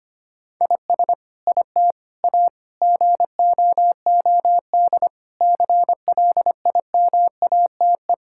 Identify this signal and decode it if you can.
Morse code, heard as a keyed tone, about 25 wpm, 700 Hz. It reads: IS IT A GOOD CLIMATE
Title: Sentences from Top 1600 Words 25wpm